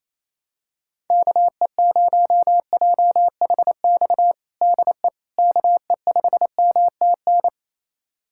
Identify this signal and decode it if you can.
Morse code, heard as a keyed tone, about 28 wpm, 700 Hz. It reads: KE0JHX DE KE5MTN
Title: USA Callsign DE Callsign - 6 Characters - 28wpm